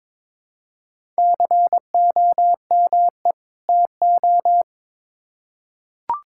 Morse code, heard as a keyed tone, about 22 wpm, 700 Hz. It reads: COME TO
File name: Build A Phrase - 2-Word Phrases - 22wpm